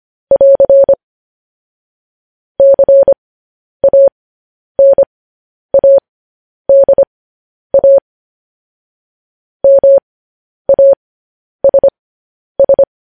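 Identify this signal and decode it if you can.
Morse code, heard as a keyed tone, about 25 wpm, spaced as for 10 wpm, 550 Hz. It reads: <AR> CANADA MASS
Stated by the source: Morse code news headlines practice at 10 wpm.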